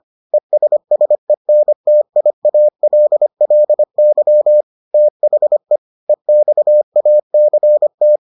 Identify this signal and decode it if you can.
Morse code, heard as a keyed tone, about 25 wpm, 600 Hz. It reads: ESSENTIALLY THE EXACT